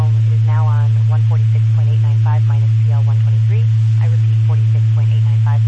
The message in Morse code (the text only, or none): none